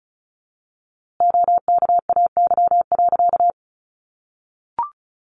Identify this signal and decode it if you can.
Morse code, heard as a keyed tone, about 34 wpm, 700 Hz. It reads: OKAY.
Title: A Sound Of Thunder 35wpm